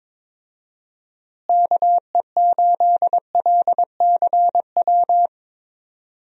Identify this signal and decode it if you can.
Morse code, heard as a keyed tone, about 22 wpm, 700 Hz. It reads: KE8LCW
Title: USA Call Signs - 6 Characters - Repeated 2x - 22wpm